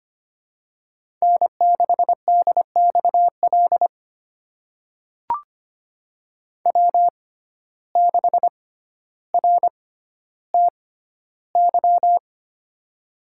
Morse code, heard as a keyed tone, about 25 wpm, spaced as for 13 wpm, 700 Hz. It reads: N6DXL W6RTY
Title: USA Call Signs - 5 Characters - 6x Character Spacing - 25wpm